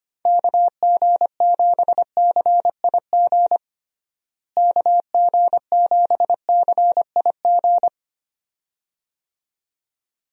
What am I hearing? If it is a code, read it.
Morse code, heard as a keyed tone, about 25 wpm, 700 Hz. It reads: KG7CIG KG7CIG
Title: USA Call Signs - 6 Characters - Repeated 2x - 25wpm